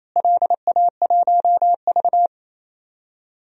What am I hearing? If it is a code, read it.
Morse code, heard as a keyed tone, about 28 wpm, 700 Hz. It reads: LA1V
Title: DXCC-100 Call Signs - 4 Characters - Repeated 2x - 28wpm